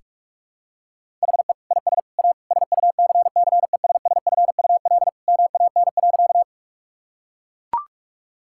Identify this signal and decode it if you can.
Morse code, heard as a keyed tone, about 42 wpm, 700 Hz. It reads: HE IS A SUCCESSFUL MAN.
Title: Sentences from Top 1000 Words - Review Part 6 - 45wpm